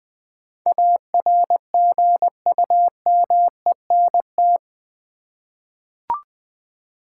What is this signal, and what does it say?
Morse code, heard as a keyed tone, about 20 wpm, 700 Hz. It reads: ARGUMENT
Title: Eight Letter Words - 20wpm